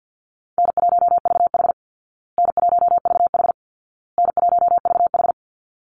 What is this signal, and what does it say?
Morse code, heard as a keyed tone, about 50 wpm, 700 Hz. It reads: N1VH N1VH N1VH
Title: Easy Call Signs Repeated 3x - 50wpm